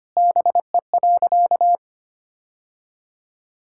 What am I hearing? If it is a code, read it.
Morse code, heard as a keyed tone, about 25 wpm, 700 Hz. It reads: BE.
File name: practice / 25wpm / Sentences with 2-Word Phrases - 25wpm